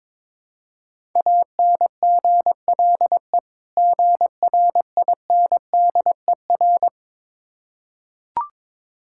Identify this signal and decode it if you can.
Morse code, heard as a keyed tone, about 23 wpm, 690 Hz. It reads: ANGLE GRINDER
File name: Difficult Words with Hints - 22wpm